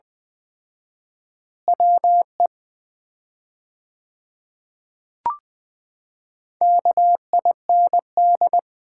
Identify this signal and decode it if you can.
Morse code, heard as a keyed tone, about 20 wpm, 690 Hz. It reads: WE KIND
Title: 4x Word-Spacing - Sets of 2 Words - 20wpm